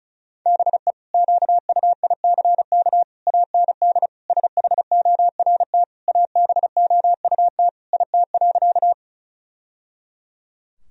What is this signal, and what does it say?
Morse code, heard as a keyed tone, about 36 wpm, 700 Hz. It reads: BE QUICK AND SHORT ABOUT IT.
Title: Sentences from Top 1000 Words - Review Part 2 - 35wpm